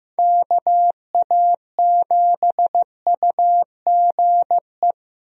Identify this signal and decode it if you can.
Morse code, heard as a keyed tone, about 15 wpm, 700 Hz. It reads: KA7UGE
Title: USA Call Signs - 6 Characters - Repeated 3x - 15wpm